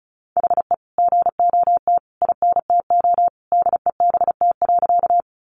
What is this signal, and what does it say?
Morse code, heard as a keyed tone, about 34 wpm, 700 Hz. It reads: HE GOT INTO DEBT.
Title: Sentences from Top 1300 Words 35wpm